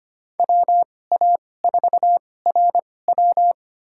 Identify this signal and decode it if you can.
Morse code, heard as a keyed tone, about 25 wpm, spaced as for 18 wpm, 700 Hz. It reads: WA4RW
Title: USA Call Signs - 5 Characters - 2x Character Spacing - 25wpm